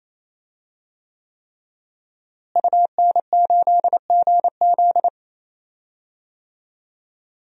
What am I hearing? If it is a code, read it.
Morse code, heard as a keyed tone, about 28 wpm, 700 Hz. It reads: UN8GZ
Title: DXCC-100 Call Signs - 5 Characters - Repeated 3x - 28wpm